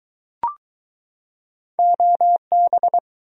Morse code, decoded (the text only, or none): OB